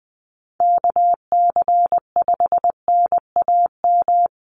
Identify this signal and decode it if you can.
Morse code, heard as a keyed tone, about 20 wpm, 700 Hz. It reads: KC5NAM
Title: USA Call Signs 20wpm